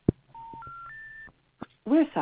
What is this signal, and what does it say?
Special information tone, with speech.